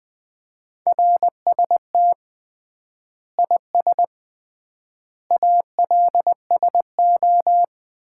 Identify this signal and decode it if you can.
Morse code, heard as a keyed tone, about 21 wpm, 700 Hz. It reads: RST IS ALSO
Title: Common QSO Phrases - 3x Word-Spacing - 20wpm